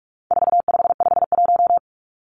Morse code, heard as a keyed tone, about 44 wpm, 700 Hz. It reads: 4551
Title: Numbers - 4-Digit 45wpm